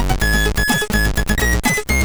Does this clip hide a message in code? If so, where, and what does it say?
No.